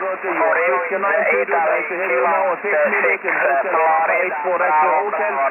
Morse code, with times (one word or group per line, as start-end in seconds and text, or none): none